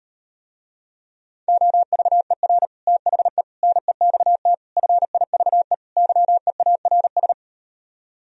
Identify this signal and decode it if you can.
Morse code, heard as a keyed tone, about 38 wpm, 700 Hz. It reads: OVER THE NEXT FIVE YEARS